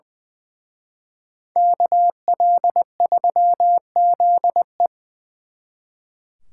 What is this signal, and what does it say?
Morse code, heard as a keyed tone, about 23 wpm, 700 Hz. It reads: KL3ZE
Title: Speed Racing - USA 2x2 Call Signs 20wpm